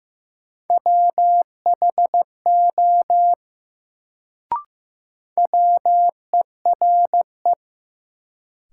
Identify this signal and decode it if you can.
Morse code, heard as a keyed tone, about 15 wpm, 700 Hz. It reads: WHO WERE
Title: Sentences from Top 100 Words - New Word Drill - 15wpm